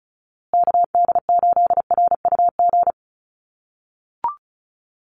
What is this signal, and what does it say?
Morse code, heard as a keyed tone, about 34 wpm, 700 Hz. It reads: KD8RUG
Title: USA Call Signs Repeated 2x 35wpm